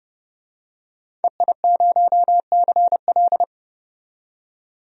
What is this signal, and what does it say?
Morse code, heard as a keyed tone, about 30 wpm, 700 Hz. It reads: EI0CL
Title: DXCC-100 Call Signs - 5 Characters - 30wpm